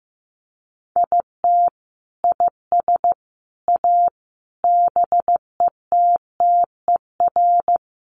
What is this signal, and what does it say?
Morse code, heard as a keyed tone, about 15 wpm, 700 Hz. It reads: IT IS A BETTER
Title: Sentences from Top 1500 Words 15wpm